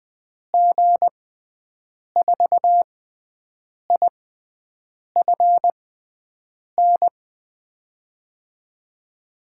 Morse code, heard as a keyed tone, about 20 wpm, spaced as for 7 wpm, 700 Hz. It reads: G4IFN
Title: USA Call Signs - 6 Characters - 6x Character Spacing - 20wpm